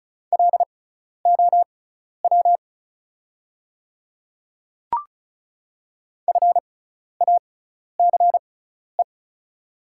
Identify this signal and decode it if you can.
Morse code, heard as a keyed tone, about 36 wpm, spaced as for 12 wpm, 690 Hz. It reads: LOW FACE
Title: 6x Character Spacing - Top 500 Words - 35wpm